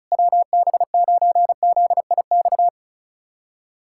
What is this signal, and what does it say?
Morse code, heard as a keyed tone, about 34 wpm, 700 Hz. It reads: WB9ZIX
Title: USA Callsign DE Callsign - 6 Characters - 35wpm